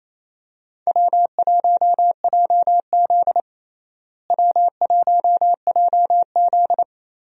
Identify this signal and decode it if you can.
Morse code, heard as a keyed tone, about 28 wpm, 700 Hz. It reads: W1JZ W1JZ
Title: USA Call Signs - 4 Characters - Repeated 3x - 28wpm